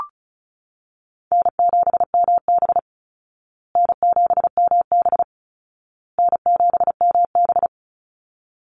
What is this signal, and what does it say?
Morse code, heard as a keyed tone, about 34 wpm, 700 Hz. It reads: N7MB N7MB N7MB
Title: Easy Call Signs Repeated 3x - 35wpm